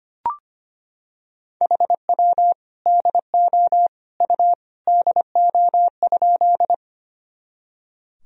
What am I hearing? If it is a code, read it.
Morse code, heard as a keyed tone, about 25 wpm, 700 Hz. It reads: HW DO U DO?
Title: Sentences from Top 100 Words and Abbreviations 25wpm